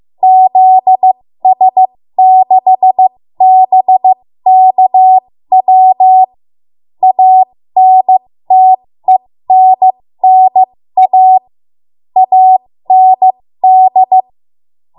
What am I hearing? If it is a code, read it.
Morse code, heard as a keyed tone, about 15 wpm, spaced as for 13 wpm, 750 Hz. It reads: ZS6BKW ANTENNA AND
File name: Morse / 13WPM